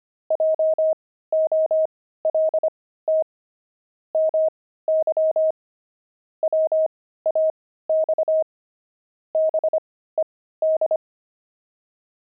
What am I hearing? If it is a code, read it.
Morse code, heard as a keyed tone, about 25 wpm, spaced as for 15 wpm, 630 Hz. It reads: JOLT MY WAX BED
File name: pangram-fs-15-cs-25-five